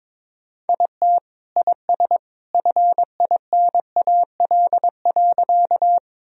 Morse code, heard as a keyed tone, about 22 wpm, 700 Hz. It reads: IT IS FINAL.
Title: Sentences from Top 1000 Words - Review Part 2 - 22wpm